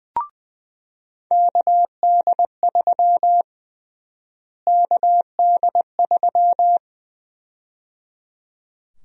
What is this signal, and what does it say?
Morse code, heard as a keyed tone, about 20 wpm, 700 Hz. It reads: KD3 KD3
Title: USA Call Sign Prefixes - Repeated 2x - 20wpm